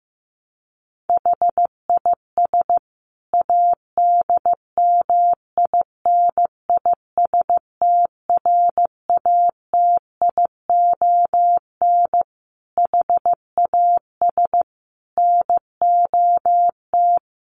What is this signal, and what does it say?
Morse code, heard as a keyed tone, about 15 wpm, 700 Hz. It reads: HIS ADMINISTRATION HAS NOT
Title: Sentences from Top 1300 Words 15wpm